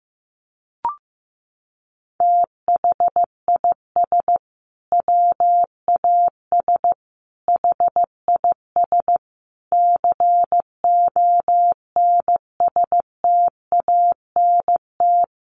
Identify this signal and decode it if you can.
Morse code, heard as a keyed tone, about 15 wpm, 700 Hz. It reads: THIS WAS HIS CONSTANT
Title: Sentences from Top 1500 Words 15wpm